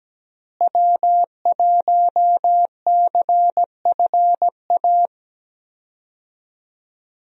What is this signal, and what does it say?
Morse code, heard as a keyed tone, about 17 wpm, 700 Hz. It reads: W1CFA
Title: USA Call Signs - 5 Characters - Repeated 3x - 17wpm